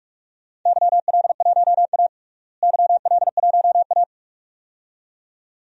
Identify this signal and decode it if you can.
Morse code, heard as a keyed tone, about 41 wpm, 700 Hz. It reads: YL1A YL1A
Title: DXCC-100 Call Signs - 4 Characters - Repeated 2x - 45wpm